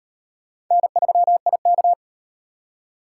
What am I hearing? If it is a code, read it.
Morse code, heard as a keyed tone, about 38 wpm, 700 Hz. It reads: N3IK